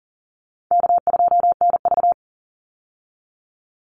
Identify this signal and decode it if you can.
Morse code, heard as a keyed tone, about 40 wpm, 700 Hz. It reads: K2NV
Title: Easy Call Signs - 40wpm